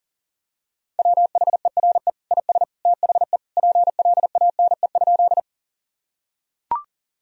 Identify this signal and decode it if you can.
Morse code, heard as a keyed tone, about 37 wpm, 700 Hz. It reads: WHERE IS THE PLANE?
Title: Sentences from Top 1000 Words - Review Part 3 - 40wpm